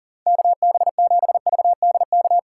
Morse code, heard as a keyed tone, about 49 wpm, 700 Hz. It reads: KB7VDK
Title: Speed Racing - USA 2x3 Call Signs 40wpm